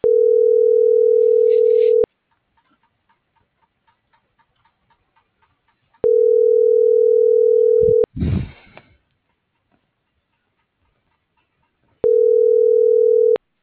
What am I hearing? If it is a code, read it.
Ringback tone (North America), 440 + 480 Hz.
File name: CENTRAL MABE / out sebastian.espinosa